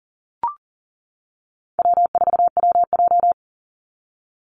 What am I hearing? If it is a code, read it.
Morse code, heard as a keyed tone, about 40 wpm, 700 Hz. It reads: W4WJ
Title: Easy Call Signs - 40wpm